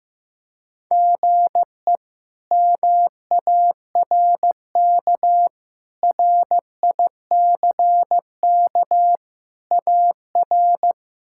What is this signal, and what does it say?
Morse code, heard as a keyed tone, about 16 wpm, 700 Hz. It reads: GE MARK RICK AR
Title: K1USN SST QSO Practice - 15wpm